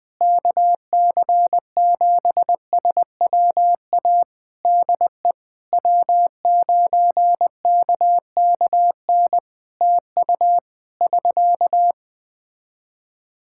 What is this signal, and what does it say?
Morse code, heard as a keyed tone, about 21 wpm, 700 Hz. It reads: KC7SWA DE W9KKN TU <SK>
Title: Practice QSO - Exchange 4 - Ending - 20wpm